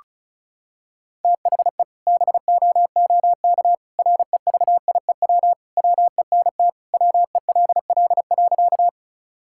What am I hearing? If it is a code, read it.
Morse code, heard as a keyed tone, about 36 wpm, 700 Hz. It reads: THE BOOK REVIEW WENT WELL.
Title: Sentences from Top 1000 Words - Review Part 6 - 35wpm